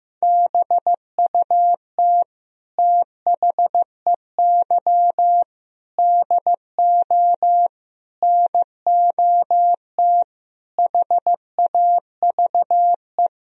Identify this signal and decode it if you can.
Morse code, heard as a keyed tone, about 15 wpm, 700 Hz. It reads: BUT THEY DO NOT HAVE
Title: Sentences from Top 1000 Words - Review Part 4 - 15wpm